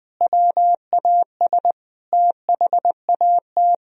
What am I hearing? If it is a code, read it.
Morse code, heard as a keyed tone, about 20 wpm, 700 Hz. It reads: WAS THAT
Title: Sentences from Top 1000 Words - Review Part 1 - 20wpm